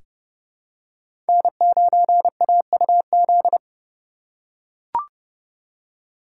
Morse code, heard as a keyed tone, about 31 wpm, 700 Hz. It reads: N9AUZ
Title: USA Call Signs - 5 Characters - 4x Character Spacing - 30wpm